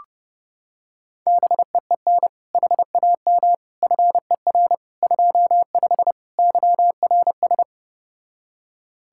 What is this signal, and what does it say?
Morse code, heard as a keyed tone, about 30 wpm, 700 Hz. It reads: BEEN HAM FER 25 YRS